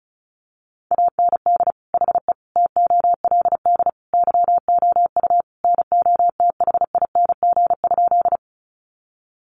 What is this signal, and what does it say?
Morse code, heard as a keyed tone, about 34 wpm, 700 Hz. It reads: AND HE TOLD YOU NOTHING?
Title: Sentences from Top 500 Words 35wpm